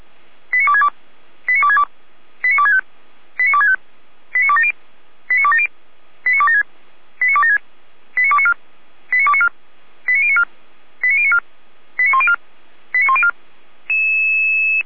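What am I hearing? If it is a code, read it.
Five-tone selective call (ZVEI), heard as quick runs of five tones, tones of about 70 ms. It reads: 89262 89262 89265 89265 89270 89270 89276 89276 89294 89294 89094 89094 89104 89104
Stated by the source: Five-tone ZVEI selective call received by SDR.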